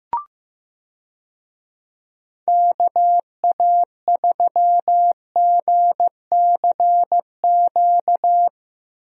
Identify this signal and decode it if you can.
Morse code, heard as a keyed tone, about 15 wpm, 700 Hz. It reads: KA3GCQ
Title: USA Call Signs - 6 Characters - Repeated 3x - 15wpm